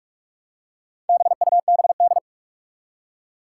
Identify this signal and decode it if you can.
Morse code, heard as a keyed tone, about 46 wpm, 700 Hz. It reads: BUBD
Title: Four Challenging Letter-Number Combinations 45wpm